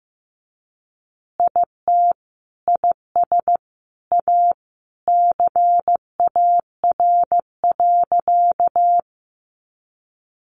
Morse code, heard as a keyed tone, about 15 wpm, 700 Hz. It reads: IT IS A CAR.
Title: Easy Phrases 15wpm